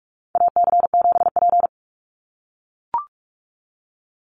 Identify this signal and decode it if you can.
Morse code, heard as a keyed tone, about 44 wpm, 700 Hz. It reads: AC7P
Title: NA SOTA Chaser Callsigns 45wpm